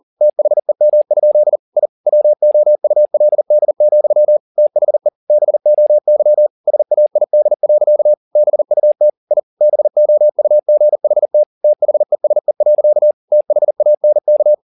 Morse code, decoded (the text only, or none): THEM? I WOULD, THE BOY SAID. BUT I BOUGHT THESE. THANK